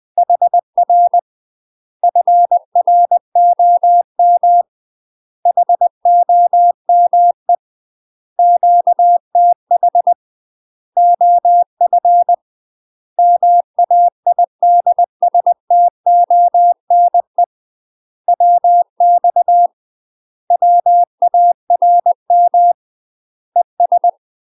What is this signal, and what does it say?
Morse code, heard as a keyed tone, about 20 wpm, 700 Hz. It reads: HR FROM HOME QTH OF MAIDSTONE WX WARM ES
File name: UK MASTER QSOs TRAINING / 20WPM / UK MASTER QSOs 0025